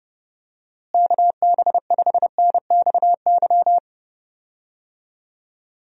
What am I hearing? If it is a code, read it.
Morse code, heard as a keyed tone, about 29 wpm, 700 Hz. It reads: KB5NXY
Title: USA Call Signs - 6 Characters - Repeated 3x - 30wpm